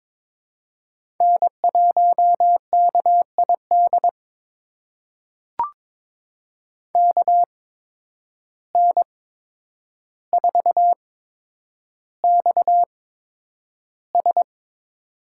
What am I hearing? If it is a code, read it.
Morse code, heard as a keyed tone, about 23 wpm, spaced as for 10 wpm, 700 Hz. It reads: N1KID KN4XS
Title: USA Call Signs - 5 Characters - 8x Character Spacing - 22wpm